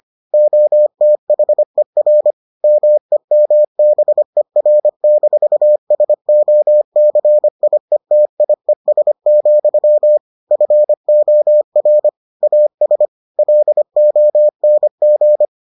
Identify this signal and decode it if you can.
Morse code, heard as a keyed tone, about 25 wpm, 600 Hz. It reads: OTHER MEMBER-SOCIETIES, FOR AS LONG